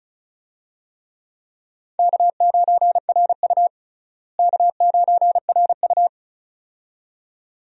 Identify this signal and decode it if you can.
Morse code, heard as a keyed tone, about 36 wpm, 700 Hz. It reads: K9RU K9RU
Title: USA Call Signs - 4 Characters - Repeated 2x - 35wpm